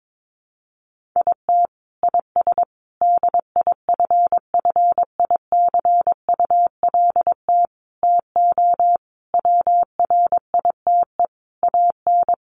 Morse code, heard as a keyed tone, about 22 wpm, 700 Hz. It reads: IT IS DIFFICULT TO WRITE AN
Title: Sentences from Top 1500 Words 22wpm